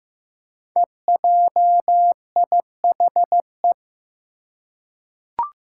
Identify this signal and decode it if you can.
Morse code, heard as a keyed tone, about 15 wpm, 700 Hz. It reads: EJIHE
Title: Five Challenging Letter-Number Combinations 15wpm